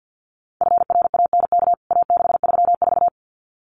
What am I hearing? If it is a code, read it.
Morse code, heard as a keyed tone, about 49 wpm, 700 Hz. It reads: FRANK A634
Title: CWT Exchange 50wpm